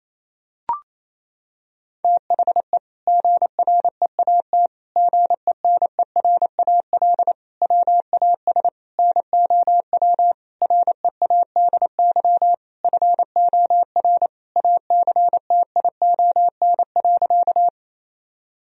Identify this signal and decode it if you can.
Morse code, heard as a keyed tone, about 28 wpm, 700 Hz. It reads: THE GREAT GENERAL WAS NOW READY FOR ACTION.
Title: Sentences from Top 1000 Words - Review Part 4 - 28wpm